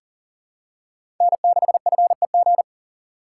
Morse code, heard as a keyed tone, about 40 wpm, 700 Hz. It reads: N6FEG
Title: USA Call Signs - 5 Characters - 4x Character Spacing - 40wpm